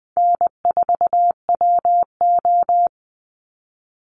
Morse code, heard as a keyed tone, about 20 wpm, 700 Hz. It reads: N4WO
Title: Easy Call Signs - 20wpm